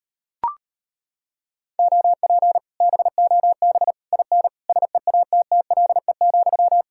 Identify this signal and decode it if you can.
Morse code, heard as a keyed tone, about 33 wpm, 700 Hz. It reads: OP BOB IN SEATTLE,